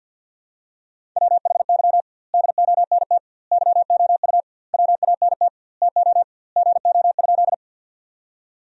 Morse code, heard as a keyed tone, about 47 wpm, 700 Hz. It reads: WHY DONT YOU WANT TO GO?
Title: 2x Word-Spacing - Sentences from Top 300 Words - 50wpm